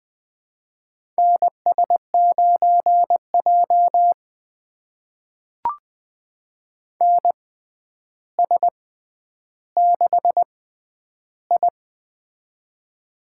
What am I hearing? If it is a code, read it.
Morse code, heard as a keyed tone, about 20 wpm, spaced as for 10 wpm, 700 Hz. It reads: NS9J NS6I
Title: USA Call Signs - 4 Characters - 6x Character Spacing - 20wpm